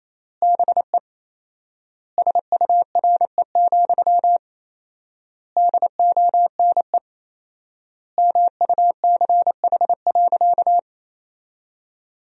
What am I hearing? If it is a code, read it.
Morse code, heard as a keyed tone, about 28 wpm, 700 Hz. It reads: BE SURE, DONE MUCH.
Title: 4x Word-Spacing - Sentences from Top 500 Words - 28wpm